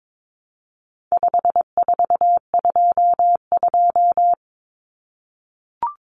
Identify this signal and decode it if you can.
Morse code, heard as a keyed tone, about 22 wpm, 700 Hz. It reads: 5422
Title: Numbers - 4-Digit 22wpm